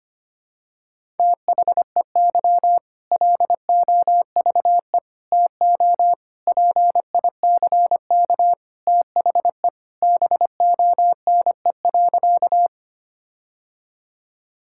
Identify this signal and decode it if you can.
Morse code, heard as a keyed tone, about 25 wpm, 700 Hz. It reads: THEY LOVE TO PICK THE BONE.
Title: Sentences from Top 1600 Words 25wpm